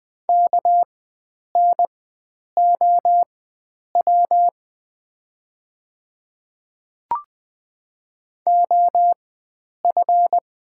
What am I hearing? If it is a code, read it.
Morse code, heard as a keyed tone, about 20 wpm, spaced as for 9 wpm, 690 Hz. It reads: KNOW OF
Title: 4x Character Spacing - Top 100 Words - 20wpm